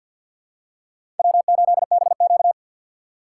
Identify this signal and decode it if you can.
Morse code, heard as a keyed tone, about 52 wpm, 700 Hz. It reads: W8BQ
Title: USA Call Signs - 4 Characters - 3x Character Spacing - 50wpm